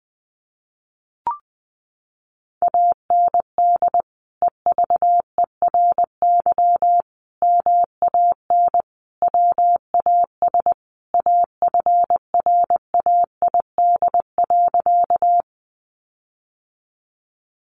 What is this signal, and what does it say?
Morse code, heard as a keyed tone, about 20 wpm, 700 Hz. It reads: AND EVERY MAN WAS AFRAID.
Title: Sentences from Top 1500 Words 20wpm